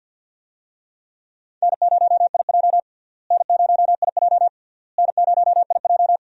Morse code, heard as a keyed tone, about 52 wpm, 700 Hz. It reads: N0IJ N0IJ N0IJ
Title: USA Call Signs - 4 Characters - Repeated 3x - 50wpm